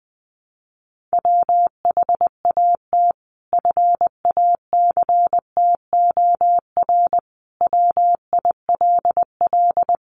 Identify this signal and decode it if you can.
Morse code, heard as a keyed tone, about 20 wpm, 700 Hz. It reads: WHAT FACTOR WILL
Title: Sentences from Top 1300 Words 20wpm